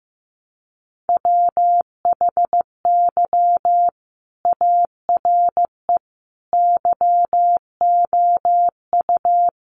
Morse code, heard as a keyed tone, about 15 wpm, 700 Hz. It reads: WHY ARE YOU
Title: Sentences from Top 800 Words 15wpm